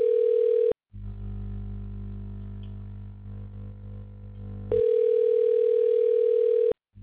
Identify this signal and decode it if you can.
Ringback tone (North America), 440 + 480 Hz.